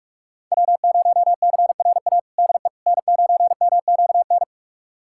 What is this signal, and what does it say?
Morse code, heard as a keyed tone, about 41 wpm, 700 Hz. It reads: W0CRA DE N9MQN
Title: USA Callsign DE Callsign - 5 Characters - 45wpm